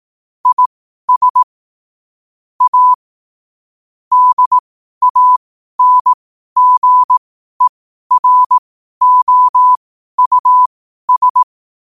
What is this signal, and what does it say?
Morse code, heard as a keyed tone, about 18 wpm, spaced as for 12 wpm, 1000 Hz. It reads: IS A DANGEROUS
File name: qotdfiles / 12wpm